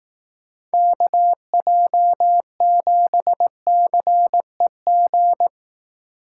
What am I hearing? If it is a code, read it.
Morse code, heard as a keyed tone, about 18 wpm, 700 Hz. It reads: KJ7CEG
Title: Speed Racing - USA Call Signs 15wpm